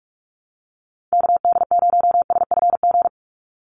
Morse code, heard as a keyed tone, about 44 wpm, 700 Hz. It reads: KD0SFG
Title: USA Call Signs 45wpm